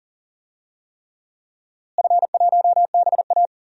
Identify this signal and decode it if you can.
Morse code, heard as a keyed tone, about 40 wpm, 700 Hz. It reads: F1BA
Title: DXCC-100 Call Signs - 4 Characters - Repeated 3x - 40wpm